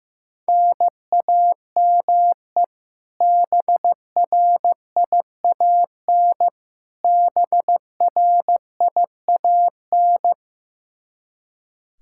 Morse code, heard as a keyed tone, about 15 wpm, 700 Hz. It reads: NAME BRIAN BRIAN
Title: K1USN SST QSO Practice 15wpm